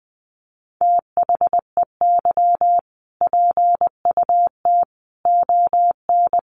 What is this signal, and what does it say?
Morse code, heard as a keyed tone, about 19 wpm, 700 Hz. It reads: THEY PUT ON
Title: A Sound Of Thunder 20wpm